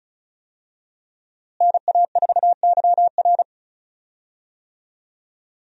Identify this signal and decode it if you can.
Morse code, heard as a keyed tone, about 36 wpm, 700 Hz. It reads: NA4YR
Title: USA Call Signs - 5 Characters - Repeated 3x - 35wpm